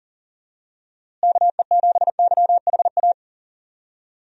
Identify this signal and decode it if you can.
Morse code, heard as a keyed tone, about 37 wpm, 700 Hz. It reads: KE7YHA
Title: USA Call Signs - 6 Characters - Repeated 3x - 40wpm